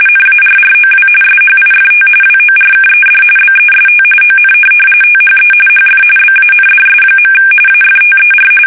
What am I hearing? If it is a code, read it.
RTTY, about 45.45 baud. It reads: NARROW SHIFT WORKED BEST ON THE HF BANDS. AFTER
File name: fsk850 / teletype-RTTY